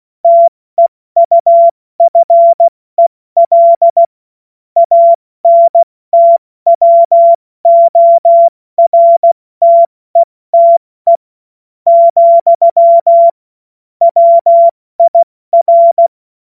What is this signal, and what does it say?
Morse code, heard as a keyed tone, about 16 wpm, spaced as for 13 wpm, 680 Hz. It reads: TEUFEL ANTWORTETE, WIR
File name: Morsekurs / Witze / Witz 30. August 2022 70 F80